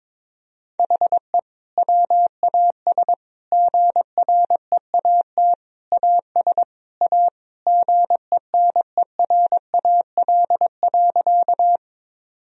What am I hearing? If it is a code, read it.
Morse code, heard as a keyed tone, about 21 wpm, 700 Hz. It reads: HE WAS GREAT AS A GENERAL.
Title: Sentences from Top 1000 Words - Review Part 3 - 22wpm